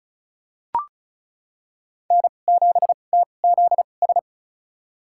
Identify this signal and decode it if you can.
Morse code, heard as a keyed tone, about 36 wpm, spaced as for 25 wpm, 700 Hz. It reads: N7TZS
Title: USA Call Signs - 5 Characters - 2x Character Spacing - 35wpm